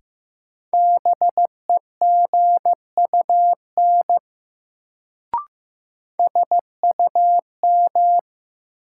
Morse code, heard as a keyed tone, about 15 wpm, 700 Hz. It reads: BEGUN SUM
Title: Sentences from Top 1000 Words - Review Part 6 - New Word Drill - 15wpm